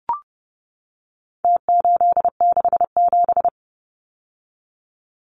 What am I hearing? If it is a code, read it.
Morse code, heard as a keyed tone, about 30 wpm, 700 Hz. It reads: T867
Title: Numbers - Contest Style - 4-Digit 30wpm